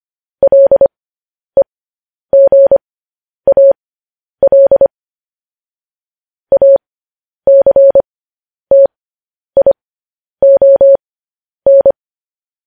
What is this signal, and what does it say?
Morse code, heard as a keyed tone, about 25 wpm, spaced as for 10 wpm, 550 Hz. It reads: LEGAL ACTION
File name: podcast / News headlines-1x-25x10